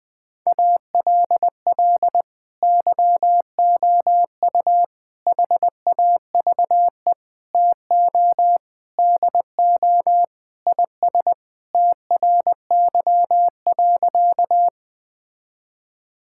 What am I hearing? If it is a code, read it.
Morse code, heard as a keyed tone, about 20 wpm, 700 Hz. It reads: ALL YOU HAVE TO DO IS TRY.
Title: Sentences with 5-Word Phrases - Spoken Beginning - 20wpm